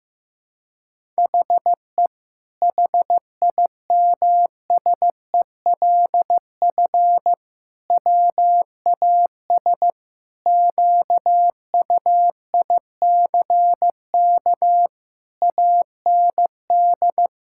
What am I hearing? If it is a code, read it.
Morse code, heard as a keyed tone, about 15 wpm, 700 Hz. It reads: HE HIMSELF WAS QUICK AND
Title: Sentences from Top 1600 Words 15wpm